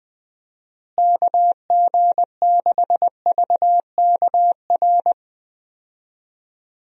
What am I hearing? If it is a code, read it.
Morse code, heard as a keyed tone, about 21 wpm, 700 Hz. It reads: KG6VKR
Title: USA Call Signs - 6 Characters - Repeated 3x - 20wpm